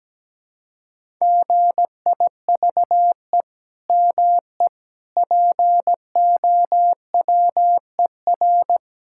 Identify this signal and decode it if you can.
Morse code, heard as a keyed tone, about 17 wpm, 700 Hz. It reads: GIVE ME POWER
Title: Sentences from Top 1000 Words - Review Part 5 - 17wpm